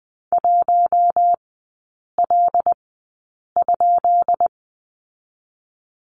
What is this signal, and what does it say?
Morse code, heard as a keyed tone, about 20 wpm, spaced as for 8 wpm, 700 Hz. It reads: 1L?
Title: 138 Lesson 23e - Letter f - Three Characters 20wpm